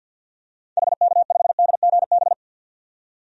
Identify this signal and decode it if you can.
Morse code, heard as a keyed tone, about 52 wpm, 700 Hz. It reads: HK5DGB
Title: DXCC-100 Call Signs - 6 Characters - 50wpm